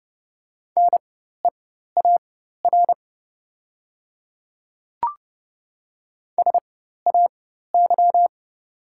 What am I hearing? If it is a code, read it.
Morse code, heard as a keyed tone, about 31 wpm, spaced as for 14 wpm, 700 Hz. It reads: NEAR SAY
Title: 4x Character Spacing - Top 200 Words - 30wpm